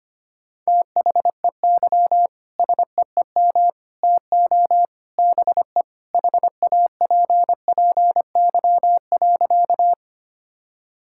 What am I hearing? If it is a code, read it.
Morse code, heard as a keyed tone, about 25 wpm, 700 Hz. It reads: THEY SEEM TO BE HAPPY.
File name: practice / 25wpm / Sentences with 3-Word Phrases - 25wpm